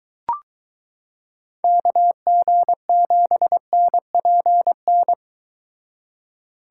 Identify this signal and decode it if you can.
Morse code, heard as a keyed tone, about 22 wpm, 700 Hz. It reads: KG7NPN